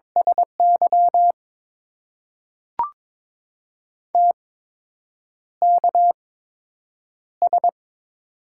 Morse code, heard as a keyed tone, about 22 wpm, spaced as for 8 wpm, 700 Hz. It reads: SY TKS
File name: Top 90 QSO Elements - 8x Character Spacing - 22wpm